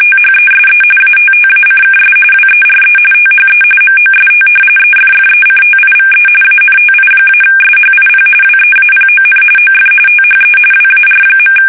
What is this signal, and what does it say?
RTTY, about 45.45 baud. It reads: THESE ATTACKS WERE THE MOST EFFECTIVE ANTI-SHIP WEAPON OF THE WAR.
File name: fsk850